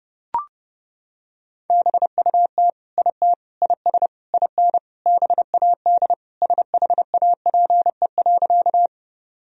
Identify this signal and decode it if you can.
Morse code, heard as a keyed tone, about 30 wpm, 700 Hz. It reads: BUT IT IS IN BAD SHAPE.
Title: Sentences from Top 1000 Words - Review Part 3 - 30wpm